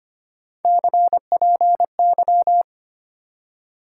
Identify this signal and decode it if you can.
Morse code, heard as a keyed tone, about 25 wpm, 700 Hz. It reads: CPY
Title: Top 90 QSO Elements 25wpm